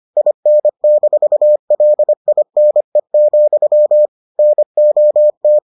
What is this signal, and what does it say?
Morse code, heard as a keyed tone, about 25 wpm, 600 Hz. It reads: IN-LINE, NOT